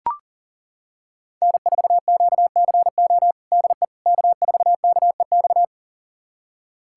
Morse code, heard as a keyed tone, about 37 wpm, 700 Hz. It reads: N4QCO DE K4KEX
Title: USA Callsign DE Callsign 40wpm